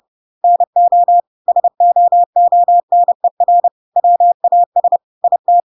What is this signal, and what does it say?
Morse code, heard as a keyed tone, about 30 wpm, 700 Hz. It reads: NO SOONER WAS IT
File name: Pinocchio / Pinocchio ch3 30 30 ik1hge